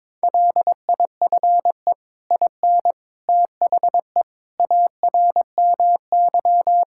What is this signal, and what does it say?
Morse code, heard as a keyed tone, about 21 wpm, 700 Hz. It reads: LIFE IN THE ARMY
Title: Sentences from Top 1000 Words - Review Part 5 - 22wpm